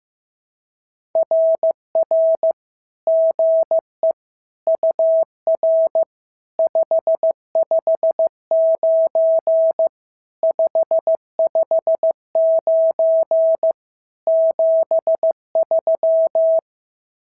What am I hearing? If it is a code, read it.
Morse code, heard as a keyed tone, about 15 wpm, 650 Hz. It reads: RR GE UR 559 559 73
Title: SOTA QSO Practice 15wpm